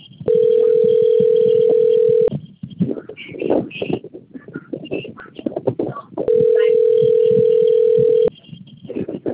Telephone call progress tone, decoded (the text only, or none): ringback tone